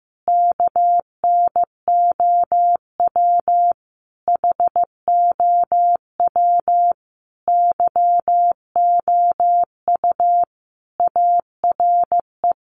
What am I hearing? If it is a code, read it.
Morse code, heard as a keyed tone, about 15 wpm, 700 Hz. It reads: KNOW HOW YOU ARE
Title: Sentences from Top 1400 Words 15wpm